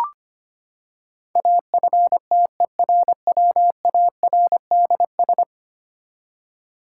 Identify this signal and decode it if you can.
Morse code, heard as a keyed tone, about 25 wpm, 700 Hz. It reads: AFTERWARDS
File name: Ten Letter Words - 25wpm